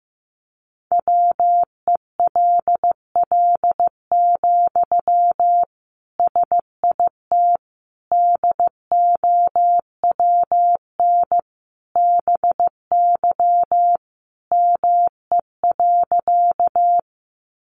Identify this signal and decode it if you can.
Morse code, heard as a keyed tone, about 15 wpm, 700 Hz. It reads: WELL, SIT DOWN BY ME.
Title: Sentences from Top 500 Words 15wpm